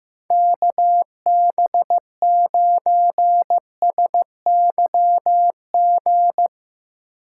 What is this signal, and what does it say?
Morse code, heard as a keyed tone, about 15 wpm, 700 Hz. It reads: KB9SYG
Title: USA Call Signs - 2x Character Spacing - 15wpm